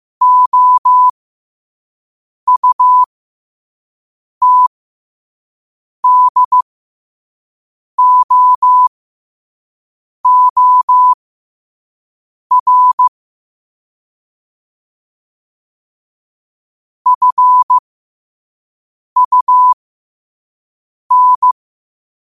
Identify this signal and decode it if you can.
Morse code, heard as a keyed tone, about 15 wpm, spaced as for 5 wpm, 1000 Hz. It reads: OUTDOOR FUN
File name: qotdfiles / 5wpm